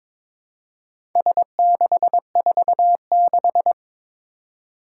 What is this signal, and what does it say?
Morse code, heard as a keyed tone, about 22 wpm, 700 Hz. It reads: S646
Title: Four Challenging Letter-Number Combinations 22wpm